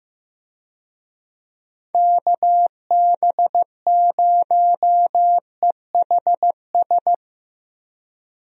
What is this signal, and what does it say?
Morse code, heard as a keyed tone, about 16 wpm, 700 Hz. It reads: KB0EHS
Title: USA Call Signs - 6 Characters - Repeated 2x - 15wpm